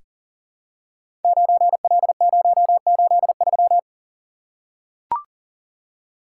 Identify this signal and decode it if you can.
Morse code, heard as a keyed tone, about 38 wpm, 700 Hz. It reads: 9L083
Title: Code Groups - 5 Letter:Numbers - 40wpm